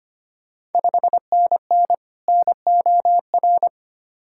Morse code, heard as a keyed tone, about 25 wpm, 690 Hz. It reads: 5NN NOR
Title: Ontario QSO Party Exchange 25wpm